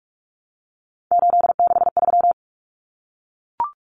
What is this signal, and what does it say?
Morse code, heard as a keyed tone, about 44 wpm, 700 Hz. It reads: 863
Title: Common Cut Numbers - 3-Digit 45wpm